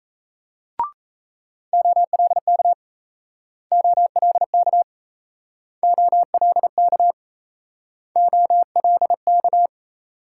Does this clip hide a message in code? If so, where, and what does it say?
Yes: Morse code from 1.7 to 9.7 s, heard as a keyed tone, about 36 wpm, 700 Hz. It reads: OLK OLK OLK OLK